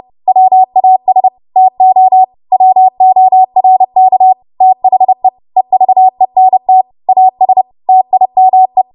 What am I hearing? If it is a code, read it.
Morse code, heard as a keyed tone, about 30 wpm, 750 Hz. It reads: WAS TO WORK THE EVENT AS TIME
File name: Morse / 30WPM